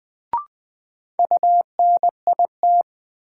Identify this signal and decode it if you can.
Morse code, heard as a keyed tone, about 20 wpm, 700 Hz. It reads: UNIT
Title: Sentences from Top 1000 Words - Review Part 2 - New Word Drill - 20wpm